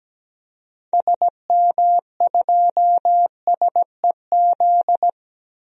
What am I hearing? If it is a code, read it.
Morse code, heard as a keyed tone, about 17 wpm, 700 Hz. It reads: SM2SEZ
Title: DXCC-100 Call Signs - 6 Characters - 17wpm